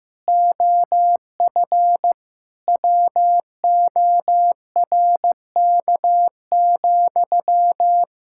Morse code, heard as a keyed tone, about 15 wpm, 700 Hz. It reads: OF WORK,
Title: Sentences from Top 1000 Words - Review Part 6 - 15wpm